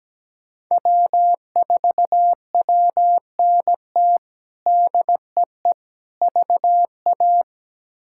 Morse code, heard as a keyed tone, about 17 wpm, 700 Hz. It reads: W4WNT DEE VA
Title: K1USN SST QSO Practice - 17wpm